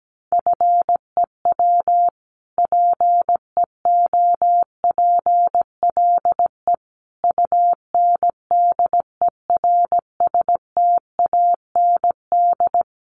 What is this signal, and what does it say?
Morse code, heard as a keyed tone, about 17 wpm, 700 Hz. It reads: FEW PEOPLE UNDERSTAND
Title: Sentences from Top 600 Words 17wpm